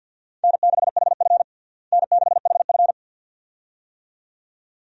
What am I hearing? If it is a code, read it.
Morse code, heard as a keyed tone, about 52 wpm, 700 Hz. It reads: N6HF N6HF
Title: USA Call Signs - 4 Characters - Repeated 2x - 50wpm